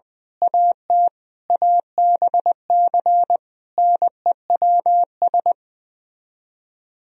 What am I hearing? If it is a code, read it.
Morse code, heard as a keyed tone, about 20 wpm, 700 Hz. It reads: AT ABC NEWS
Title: Speed-Racing - Sets of 5 Words 17wpm